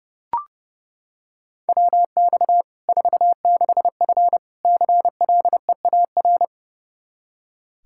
Morse code, heard as a keyed tone, about 32 wpm, 700 Hz. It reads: WX 46F CLEAR
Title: Practice QSO - Exchange 2 - Info - 30wpm